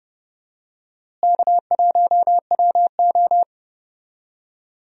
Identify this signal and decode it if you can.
Morse code, heard as a keyed tone, about 32 wpm, 700 Hz. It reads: K1WO
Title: USA Call Signs - 4 Characters - Repeated 2x - 30wpm